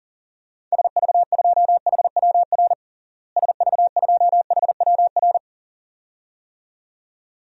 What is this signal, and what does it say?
Morse code, heard as a keyed tone, about 40 wpm, 700 Hz. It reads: SV2HWR SV2HWR
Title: DXCC-100 Call Signs - 6 Characters - Repeated 2x - 40wpm